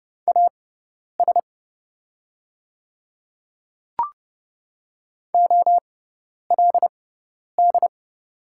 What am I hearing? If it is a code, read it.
Morse code, heard as a keyed tone, about 30 wpm, spaced as for 10 wpm, 700 Hz. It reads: AS OLD
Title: 6x Character Spacing - Top 300 Words - 30wpm